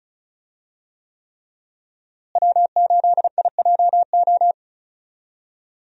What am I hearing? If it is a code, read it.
Morse code, heard as a keyed tone, about 36 wpm, 700 Hz. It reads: W8IJO
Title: USA Call Signs - 5 Characters - Repeated 3x - 35wpm